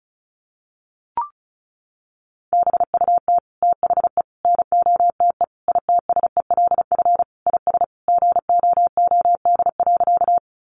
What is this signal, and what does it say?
Morse code, heard as a keyed tone, about 34 wpm, 700 Hz. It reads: BUT THE NOTE ITSELF IS GOOD.
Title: Sentences from Top 800 Words 35wpm